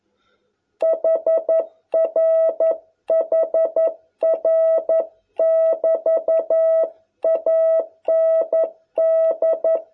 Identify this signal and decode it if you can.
Morse code, heard as a keyed tone, about 11 wpm, 660 Hz. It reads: HR HR <BT> AND